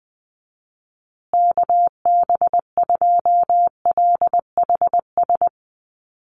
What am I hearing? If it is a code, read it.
Morse code, heard as a keyed tone, about 19 wpm, 700 Hz. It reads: KB2LHS
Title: USA Call Signs Repeated 3x 20wpm